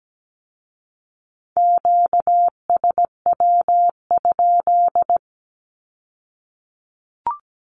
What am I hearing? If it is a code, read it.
Morse code, heard as a keyed tone, about 17 wpm, 700 Hz. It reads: QSW?
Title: Q-Codes with Spoken Meaning - 17wpm